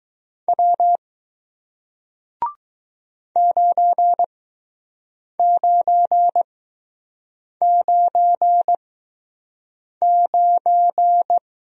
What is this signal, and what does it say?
Morse code, heard as a keyed tone, about 20 wpm, spaced as for 6 wpm, 700 Hz. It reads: W 9999